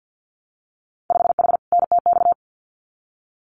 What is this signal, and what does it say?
Morse code, heard as a keyed tone, about 50 wpm, 700 Hz. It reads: 5H NTX
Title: Winter Field Day Exchanges Encore 50wpm